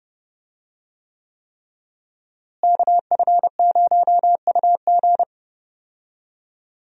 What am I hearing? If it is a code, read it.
Morse code, heard as a keyed tone, about 30 wpm, 700 Hz. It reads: KF0UG
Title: USA Call Signs - 5 Characters - Repeated 3x - 30wpm